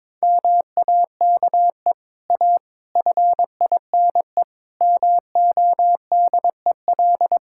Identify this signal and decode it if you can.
Morse code, heard as a keyed tone, about 22 wpm, 700 Hz. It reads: MAKE A FINE MODEL
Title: Sentences from Top 1000 Words - Review Part 5 - 22wpm